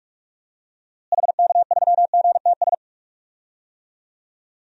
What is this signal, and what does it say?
Morse code, heard as a keyed tone, about 41 wpm, 700 Hz. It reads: HK3GTS
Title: DXCC-100 Call Signs - 6 Characters - Repeated 3x - 45wpm